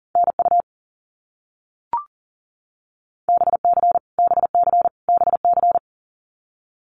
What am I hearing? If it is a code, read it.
Morse code, heard as a keyed tone, about 40 wpm, 700 Hz. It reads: NU BC BC BC
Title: Canadian Provinces and Territories - Repeated 3x - 40wpm